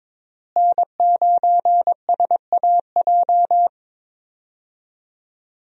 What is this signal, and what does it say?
Morse code, heard as a keyed tone, about 22 wpm, 700 Hz. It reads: N9SAJ
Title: USA Call Signs - 5 Characters - Repeated 3x - 22wpm